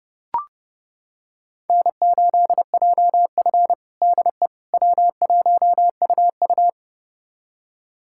Morse code, heard as a keyed tone, about 30 wpm, 700 Hz. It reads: N8JF DE W1UU
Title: USA Callsign DE Callsign - 4 Characters - 30wpm